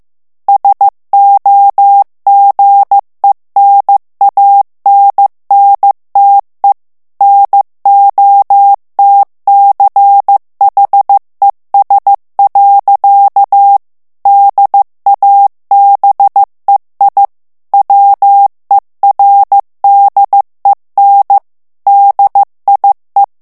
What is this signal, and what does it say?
Morse code, heard as a keyed tone, about 15 wpm, 800 Hz. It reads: SOGENANNTE NOTCHES. DABEI WERDEN DIE
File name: morse mp3 / ALLEMAND texte clair 1 vitesse 14 mots minute